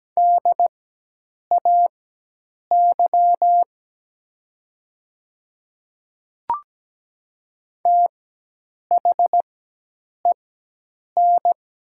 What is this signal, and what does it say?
Morse code, heard as a keyed tone, about 17 wpm, spaced as for 8 wpm, 700 Hz. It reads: DAY THEN
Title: 4x Character Spacing - Top 100 Words - 17wpm